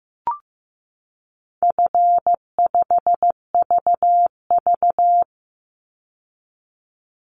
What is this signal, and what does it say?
Morse code, heard as a keyed tone, about 15 wpm, 700 Hz. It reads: F5VV
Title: Easy Call Signs Repeated 2x - 15wpm